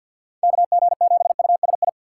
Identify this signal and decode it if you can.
Morse code, heard as a keyed tone, about 46 wpm, 690 Hz. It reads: KG7USI
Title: USA Call Signs - 6 Characters - 3x Character Spacing - 50wpm